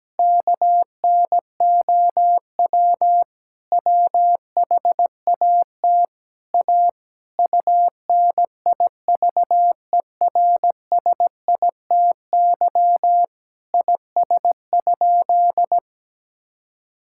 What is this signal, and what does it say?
Morse code, heard as a keyed tone, about 17 wpm, 700 Hz. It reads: KNOW WHAT A UNIVERSITY IS?
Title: Sentences from Top 1000 Words - Review Part 5 - 17wpm